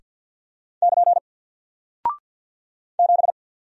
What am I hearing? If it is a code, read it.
Morse code, heard as a keyed tone, about 50 wpm, 700 Hz. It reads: <KN> 7
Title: Single Character (Letter:Number:Punctuation:Prosign) 50wpm